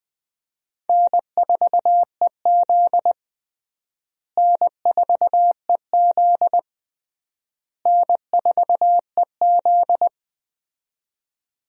Morse code, heard as a keyed tone, about 19 wpm, 700 Hz. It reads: N4EZ N4EZ N4EZ
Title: USA Call Signs - 4 Characters - Repeated 3x - 20wpm